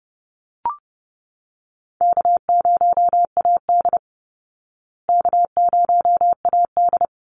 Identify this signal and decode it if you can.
Morse code, heard as a keyed tone, about 30 wpm, 700 Hz. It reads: K0AD K0AD
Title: Call Signs Repeated 3x - 30wpm